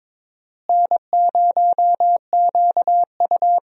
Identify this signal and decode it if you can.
Morse code, heard as a keyed tone, about 22 wpm, 700 Hz. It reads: N0QU
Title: USA Call Signs - 4 Characters - Repeated 2x - 22wpm